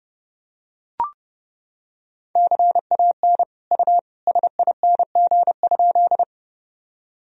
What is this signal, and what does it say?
Morse code, heard as a keyed tone, about 30 wpm, 700 Hz. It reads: CAN U SING?
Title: Sentences from Top 400 Words and Abbreviations 30wpm